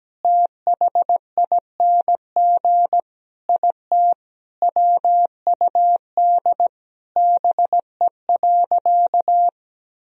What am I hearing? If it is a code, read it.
Morse code, heard as a keyed tone, about 17 wpm, 700 Hz. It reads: THING IT WUD BE.
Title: Sentences from Top 200 Words and Abbreviations 17wpm